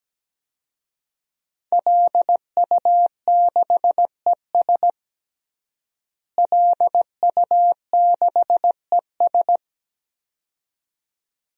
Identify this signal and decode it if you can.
Morse code, heard as a keyed tone, about 17 wpm, 700 Hz. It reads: LU6ES LU6ES
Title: DXCC-100 Call Signs - 5 Characters - Repeated 2x - 17wpm